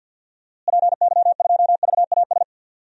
Morse code, heard as a keyed tone, about 46 wpm, 700 Hz. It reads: PY2VAS
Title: DXCC-100 Call Signs - 6 Characters - Repeated 2x - 50wpm